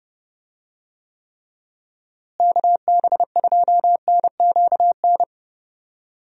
Morse code, heard as a keyed tone, about 29 wpm, 700 Hz. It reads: KB2NQN
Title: USA Call Signs - 6 Characters - Repeated 2x - 30wpm